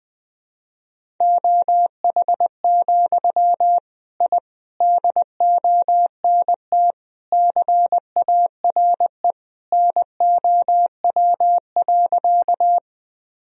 Morse code, heard as a keyed tone, about 20 wpm, 700 Hz. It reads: OH, I DONT CARE NOW.
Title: Sentences from Top 1000 Words - Review Part 1 - 20wpm